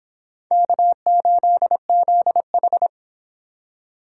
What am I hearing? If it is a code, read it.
Morse code, heard as a keyed tone, about 22 wpm, 700 Hz. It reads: K8ZH